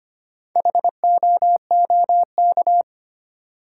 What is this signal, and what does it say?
Morse code, heard as a keyed tone, about 25 wpm, 700 Hz. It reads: HOOK
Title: Sentences from Top 1600 Words - New Word Drill Alt - 25wpm